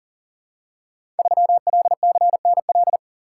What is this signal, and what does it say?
Morse code, heard as a keyed tone, about 38 wpm, 700 Hz. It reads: 3LCNL
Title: Code Groups - 5 Letter:Numbers - 40wpm